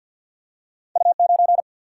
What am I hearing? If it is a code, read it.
Morse code, heard as a keyed tone, about 53 wpm, 700 Hz. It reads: U 9
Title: Single-Letter Single-Number 50wpm